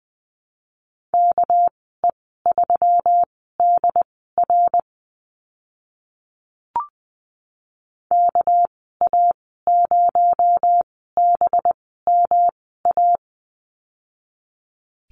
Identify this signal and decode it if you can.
Morse code, heard as a keyed tone, about 20 wpm, spaced as for 15 wpm, 700 Hz. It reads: KE3DR KA0BMA
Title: 219 Lesson 35g - Number 0 - Call Signs 20wpm